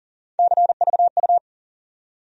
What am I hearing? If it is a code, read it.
Morse code, heard as a keyed tone, about 40 wpm, 700 Hz. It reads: CVU
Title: USA Call Sign Postfixes - 3 Letters 40wpm